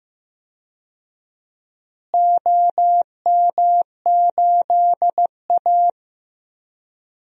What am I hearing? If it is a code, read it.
Morse code, heard as a keyed tone, about 15 wpm, 700 Hz. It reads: OM8A
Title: DXCC-100 Call Signs - 4 Characters - Repeated 2x - 15wpm